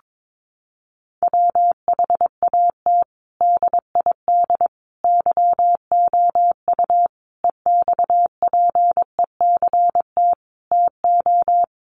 Morse code, heard as a keyed tone, about 22 wpm, 700 Hz. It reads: WHAT DID YOU EXPECT TO
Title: Sentences from Top 1100 Words 22wpm